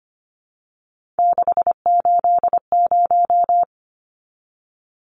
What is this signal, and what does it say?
Morse code, heard as a keyed tone, about 25 wpm, 700 Hz. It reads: 680
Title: Numbers - 3-Digit 25wpm